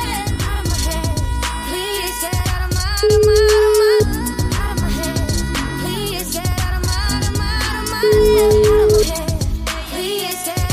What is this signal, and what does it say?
Ringback tone (Europe, 425 Hz), with noise.